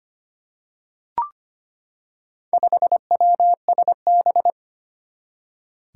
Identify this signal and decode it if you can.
Morse code, heard as a keyed tone, about 25 wpm, 700 Hz. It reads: 5WSB
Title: Four Challenging Letter-Number Combinations 25wpm